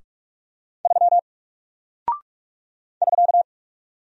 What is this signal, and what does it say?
Morse code, heard as a keyed tone, about 44 wpm, 700 Hz. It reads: 3 <SK>
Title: Single Character (Letter:Number:Punctuation:Prosign) 45wpm